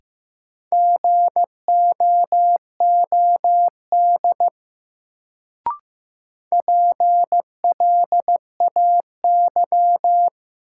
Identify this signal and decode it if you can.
Morse code, heard as a keyed tone, about 15 wpm, 700 Hz. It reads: GOOD PLAY
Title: Sentences from Top 200 Words - New Word Drill - 15wpm